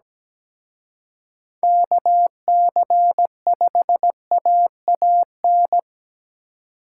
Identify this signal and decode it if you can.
Morse code, heard as a keyed tone, about 20 wpm, 700 Hz. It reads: KC5AAN
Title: Speed Racing - USA 2x3 Call Signs 17wpm